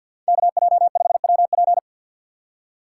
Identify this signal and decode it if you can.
Morse code, heard as a keyed tone, about 53 wpm, 700 Hz. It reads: KJ5WP
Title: USA Call Signs - 5 Characters - 8x Character Spacing - 50wpm